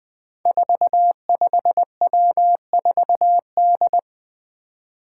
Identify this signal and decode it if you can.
Morse code, heard as a keyed tone, about 20 wpm, 700 Hz. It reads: 45W4D
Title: Five Challenging Letter-Number Combinations 20wpm